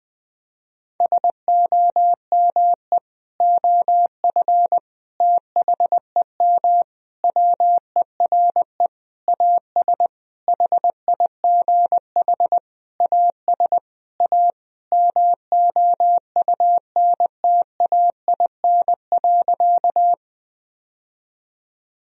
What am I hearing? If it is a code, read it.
Morse code, heard as a keyed tone, about 21 wpm, 700 Hz. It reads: SOME OF THEM WERE AS HIGH AS A MOUNTAIN.
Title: Sentences from Top 1000 Words - Review Part 1 - 20wpm